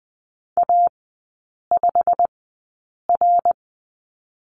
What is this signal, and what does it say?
Morse code, heard as a keyed tone, about 20 wpm, spaced as for 8 wpm, 690 Hz. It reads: A5R
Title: 212 Lesson 34e - Number 8 - Three Characters 20wpm